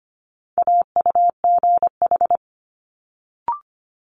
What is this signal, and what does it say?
Morse code, heard as a keyed tone, about 25 wpm, 700 Hz. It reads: AUGH
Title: Multi-letter Phonemes Quick Fire 25wpm